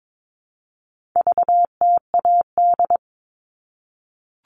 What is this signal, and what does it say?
Morse code, heard as a keyed tone, about 22 wpm, 700 Hz. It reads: VTAD
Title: All Cut Numbers - 4-Digit 22wpm